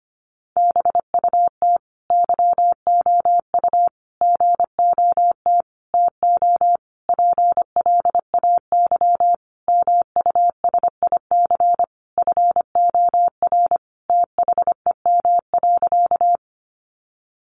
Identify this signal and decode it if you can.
Morse code, heard as a keyed tone, about 25 wpm, 700 Hz. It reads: BUT YOU GOT TO PLAY MUSIC FOR THEM.
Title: Sentences from Top 300 Words 25wpm